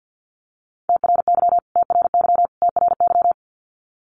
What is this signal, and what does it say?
Morse code, heard as a keyed tone, about 48 wpm, 700 Hz. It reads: TRY TRY TRY
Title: Top 300 Words Repeated 50wpm